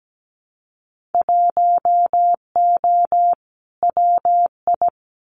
Morse code, heard as a keyed tone, about 17 wpm, 700 Hz. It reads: 1O WI
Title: Winter Field Day Exchanges 17wpm